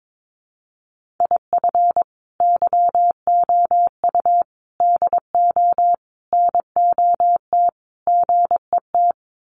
Morse code, heard as a keyed tone, about 22 wpm, 700 Hz. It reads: IF YOU DO NOT GET
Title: Sets of 5 Words Encore2 22wpm